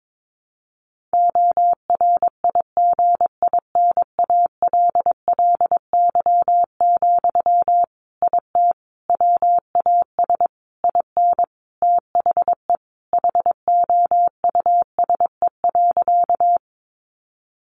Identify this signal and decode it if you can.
Morse code, heard as a keyed tone, about 22 wpm, 700 Hz. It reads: ORIGINALLY, IT WAS IN THE HOUSE.
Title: Sentences from Top 1500 Words 22wpm